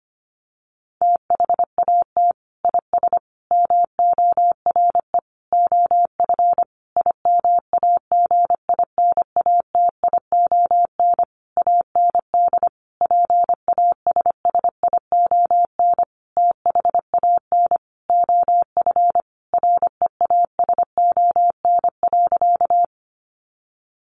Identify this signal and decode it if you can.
Morse code, heard as a keyed tone, about 25 wpm, 700 Hz. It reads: THAT IS MORE OF IMAGINATION AND PASSION THAN OF REASON.
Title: Sentences from Top 1300 Words 25wpm